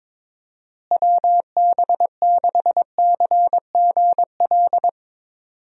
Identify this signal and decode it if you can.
Morse code, heard as a keyed tone, about 21 wpm, 700 Hz. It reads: WB6CGL
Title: USA Call Signs - 6 Characters - Repeated 2x - 22wpm